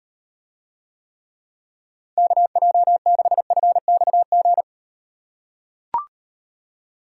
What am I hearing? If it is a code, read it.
Morse code, heard as a keyed tone, about 38 wpm, 700 Hz. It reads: KJ6FXG